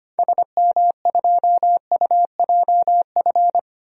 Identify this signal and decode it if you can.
Morse code, heard as a keyed tone, about 25 wpm, 700 Hz. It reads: SM2UJF
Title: DXCC-100 Call Signs - 6 Characters - Repeated 2x - 25wpm